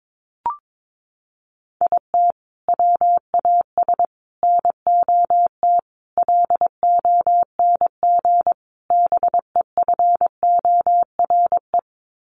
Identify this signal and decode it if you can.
Morse code, heard as a keyed tone, about 22 wpm, 700 Hz. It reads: IT WAS NOT LONG BEFORE
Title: Sets of 5 Words Encore2 22wpm